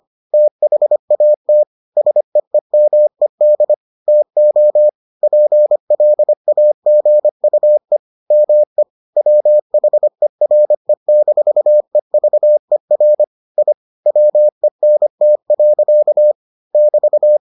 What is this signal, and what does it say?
Morse code, heard as a keyed tone, about 25 wpm, 600 Hz. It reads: THAT SEEMED TO PLAGUE ME WHERE-EVER I WENT. <BT>